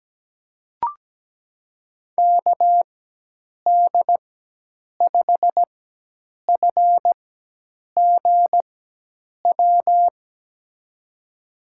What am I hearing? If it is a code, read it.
Morse code, heard as a keyed tone, about 17 wpm, spaced as for 8 wpm, 700 Hz. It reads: KD5FGW
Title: USA Call Signs - 4x Character Spacing - 17wpm